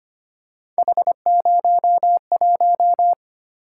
Morse code, heard as a keyed tone, about 25 wpm, 700 Hz. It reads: H01
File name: Three Letter-Number Combinations - 25wpm